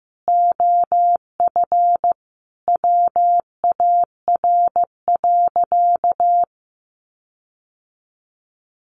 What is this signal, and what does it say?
Morse code, heard as a keyed tone, about 15 wpm, 700 Hz. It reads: OF WAR.
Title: Sentences from Top 1500 Words 15wpm